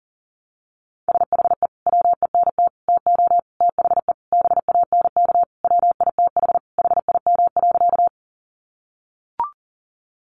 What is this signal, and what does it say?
Morse code, heard as a keyed tone, about 40 wpm, 700 Hz. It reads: SHE WENT TO THE BANK WITH HIM.
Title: Sentences from Top 900 Words 40wpm